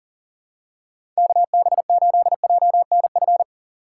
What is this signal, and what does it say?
Morse code, heard as a keyed tone, about 39 wpm, 700 Hz. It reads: KB8JNF
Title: USA Call Signs - 6 Characters - 3x Character Spacing - 40wpm